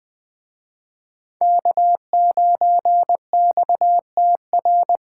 Morse code, heard as a keyed tone, about 24 wpm, 700 Hz. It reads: K9XTR
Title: Speed Racing - USA 1x3 Call Signs 20wpm